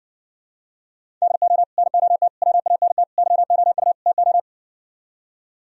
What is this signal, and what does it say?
Morse code, heard as a keyed tone, about 69 wpm, 700 Hz. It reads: DO NOT WANT YOU TO